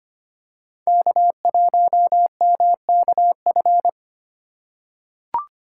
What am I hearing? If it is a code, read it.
Morse code, heard as a keyed tone, about 25 wpm, 700 Hz. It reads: K1MKF
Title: USA Call Signs - 4x Character Spacing - 25wpm